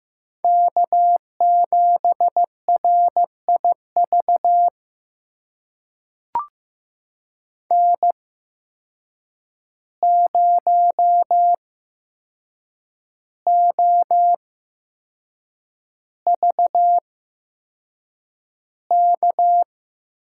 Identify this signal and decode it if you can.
Morse code, heard as a keyed tone, about 15 wpm, spaced as for 6 wpm, 700 Hz. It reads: K7RIV N0OVK
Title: USA Call Signs - 5 Characters - 8x Character Spacing - 15wpm